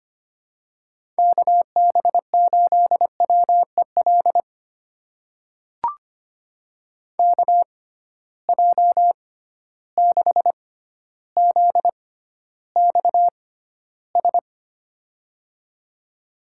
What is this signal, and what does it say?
Morse code, heard as a keyed tone, about 25 wpm, spaced as for 13 wpm, 700 Hz. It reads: KB8WEL KJ6ZXS
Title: USA Call Signs - 6 Characters - 6x Character Spacing - 25wpm